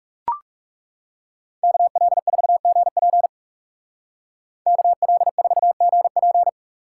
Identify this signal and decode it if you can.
Morse code, heard as a keyed tone, about 36 wpm, 700 Hz. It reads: KL4GP KL4GP
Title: Speed Racing - USA 2x2 Call Signs 30wpm